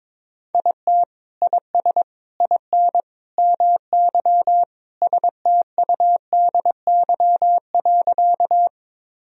Morse code, heard as a keyed tone, about 22 wpm, 700 Hz. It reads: IT IS IN MY STUDY.
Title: Sentences from Top 1000 Words - Review Part 1 - 22wpm